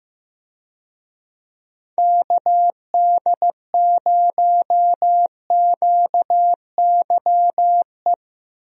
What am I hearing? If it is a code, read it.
Morse code, heard as a keyed tone, about 15 wpm, 700 Hz. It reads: KD0QYE
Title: USA Call Signs - 6 Characters - 15wpm